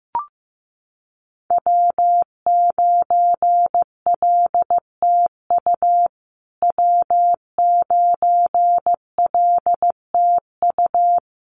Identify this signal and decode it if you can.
Morse code, heard as a keyed tone, about 15 wpm, 700 Hz. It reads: W9LTU W9LTU
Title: USA Call Signs Repeated 3x 15wpm